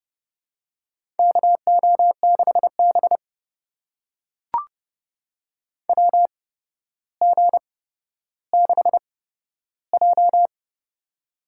Morse code, heard as a keyed tone, about 32 wpm, spaced as for 13 wpm, 700 Hz. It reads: KO6B WG6J
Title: USA Call Signs - 4 Characters - 8x Character Spacing - 30wpm